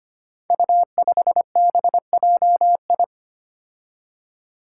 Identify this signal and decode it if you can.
Morse code, heard as a keyed tone, about 25 wpm, 700 Hz. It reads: U5BJI
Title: Five Challenging Letter-Number Combinations 25wpm